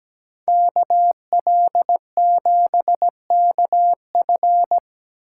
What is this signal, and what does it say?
Morse code, heard as a keyed tone, about 20 wpm, 700 Hz. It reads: KL7KF
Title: Speed Racing - USA 2x2 Call Signs 17wpm